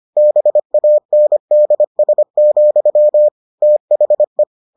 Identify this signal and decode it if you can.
Morse code, heard as a keyed tone, about 25 wpm, 600 Hz. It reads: BANDS, THE